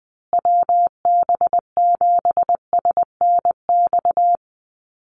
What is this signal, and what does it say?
Morse code, heard as a keyed tone, about 20 wpm, 700 Hz. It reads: WB7SNX
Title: USA Call Signs Repeated 3x 20wpm